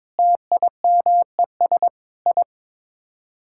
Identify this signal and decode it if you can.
Morse code, heard as a keyed tone, about 22 wpm, 700 Hz. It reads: TIMES I
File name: practice / 22wpm / Sets of 8 Words - 22wpm